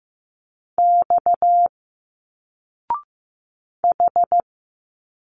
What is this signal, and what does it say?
Morse code, heard as a keyed tone, about 15 wpm, 700 Hz. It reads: X H
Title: Single Letters 15wpm